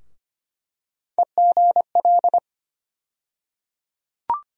Morse code, heard as a keyed tone, about 25 wpm, 700 Hz. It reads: EGL
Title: USA Call Sign Postfixes - 3 Letters 25wpm